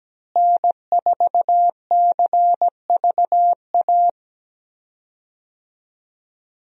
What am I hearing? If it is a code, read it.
Morse code, heard as a keyed tone, about 17 wpm, 700 Hz. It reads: N4CVA
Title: USA Call Signs - 5 Characters - Repeated 3x - 17wpm